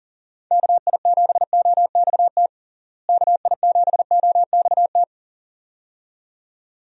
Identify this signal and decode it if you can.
Morse code, heard as a keyed tone, about 37 wpm, 700 Hz. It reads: KI7OXT KI7OXT
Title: USA Call Signs - 6 Characters - Repeated 2x - 40wpm